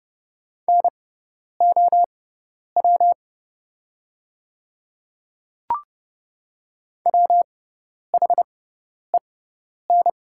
Morse code, heard as a keyed tone, about 30 wpm, spaced as for 10 wpm, 700 Hz. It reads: NOW WHEN
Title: 6x Character Spacing - Top 100 Words - 30wpm